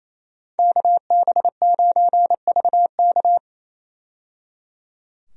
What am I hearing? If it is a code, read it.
Morse code, heard as a keyed tone, about 28 wpm, 700 Hz. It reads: KB9VK
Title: USA Call Signs - 5 Characters - 28wpm